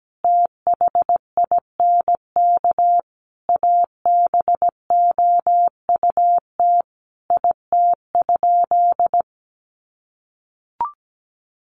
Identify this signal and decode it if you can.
Morse code, heard as a keyed tone, about 17 wpm, 700 Hz. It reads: THINK ABOUT IT?
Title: Sentences from Top 200 Words 17wpm